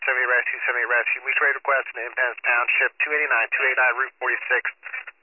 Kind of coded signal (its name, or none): Emergency Alert System (SAME) tones